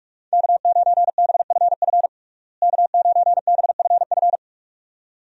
Morse code, heard as a keyed tone, about 46 wpm, 700 Hz. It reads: K9BFF K9BFF
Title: USA Call Signs - 5 Characters - Repeated 2x - 45wpm